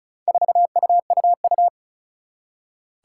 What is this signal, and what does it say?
Morse code, heard as a keyed tone, about 36 wpm, 700 Hz. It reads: 4UUU
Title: Four Challenging Letter-Number Combinations 35wpm